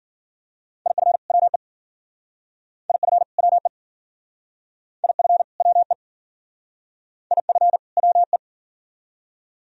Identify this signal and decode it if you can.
Morse code, heard as a keyed tone, about 50 wpm, 690 Hz. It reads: IF WE IF WE IF WE IF WE